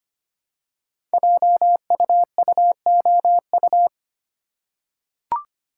Morse code, heard as a keyed tone, about 25 wpm, 700 Hz. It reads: JUUOU
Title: Code Groups - 5 Letters - 25wpm